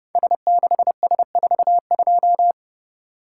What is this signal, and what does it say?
Morse code, heard as a keyed tone, about 31 wpm, 700 Hz. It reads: S6S42
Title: Code Groups - 5 Letter:Numbers - 30wpm